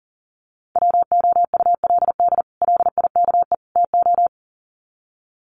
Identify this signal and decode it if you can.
Morse code, heard as a keyed tone, about 40 wpm, 700 Hz. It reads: WOULD LIKE TO
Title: Sets of 3 Words 40wpm